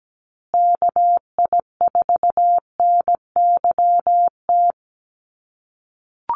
Morse code, heard as a keyed tone, about 17 wpm, 700 Hz. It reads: KI4NYT
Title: USA Call Signs Repeated 2x 17wpm